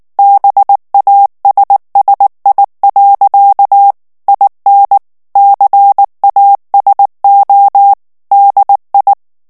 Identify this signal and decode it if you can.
Morse code, heard as a keyed tone, about 19 wpm, 800 Hz. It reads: BASSI. IN CASO DI
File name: morse mp3 / ITALIEN texte clair 2 vitesse 18 mots minute